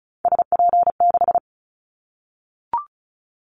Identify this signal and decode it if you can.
Morse code, heard as a keyed tone, about 34 wpm, 700 Hz. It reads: SP6
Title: Callsign Prefixes for Top 25 DXCC - 35wpm